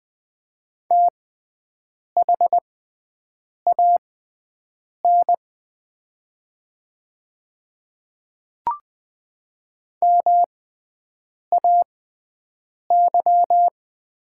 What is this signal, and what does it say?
Morse code, heard as a keyed tone, about 20 wpm, spaced as for 7 wpm, 700 Hz. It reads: THAN MAY
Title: 6x Character Spacing - Top 200 Words - 20wpm